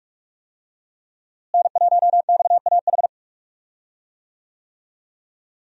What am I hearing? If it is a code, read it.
Morse code, heard as a keyed tone, about 43 wpm, 700 Hz. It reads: N1XAH
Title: USA Call Signs - 5 Characters - Repeated 2x - 45wpm